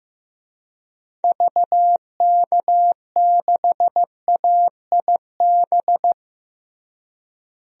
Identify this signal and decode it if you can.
Morse code, heard as a keyed tone, about 15 wpm, 700 Hz. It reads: VK6AIB
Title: DXCC-100 Call Signs - 6 Characters - Repeated 2x - 15wpm